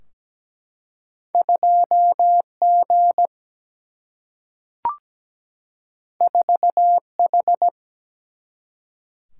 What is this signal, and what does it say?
Morse code, heard as a keyed tone, about 17 wpm, 700 Hz. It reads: 2 G 4 H
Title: Single-Letter Single-Number 17wpm